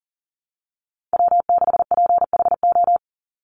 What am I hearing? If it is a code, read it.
Morse code, heard as a keyed tone, about 40 wpm, 700 Hz. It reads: W6PHO
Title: Call Signs Repeated 2x - 40wpm